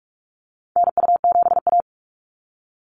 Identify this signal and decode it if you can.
Morse code, heard as a keyed tone, about 44 wpm, 700 Hz. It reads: NU7A
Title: NA SOTA Activator Callsigns 45wpm